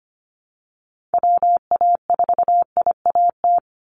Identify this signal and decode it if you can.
Morse code, heard as a keyed tone, about 25 wpm, 700 Hz. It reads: WA4IAT
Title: USA Call Signs 25wpm